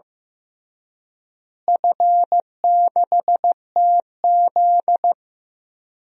Morse code, heard as a keyed tone, about 15 wpm, 700 Hz. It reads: F6TZ
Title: DXCC-100 Call Signs - 4 Characters - Repeated 3x - 15wpm